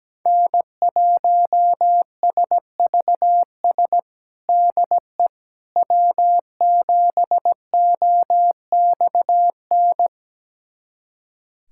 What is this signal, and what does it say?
Morse code, heard as a keyed tone, about 17 wpm, 700 Hz. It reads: N1SVS DE W7OXN
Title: USA Callsign DE Callsign - 5 Characters - 17wpm